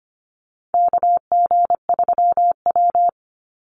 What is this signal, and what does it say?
Morse code, heard as a keyed tone, about 25 wpm, 700 Hz. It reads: KG3W
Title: NA SOTA Chaser Callsigns 25wpm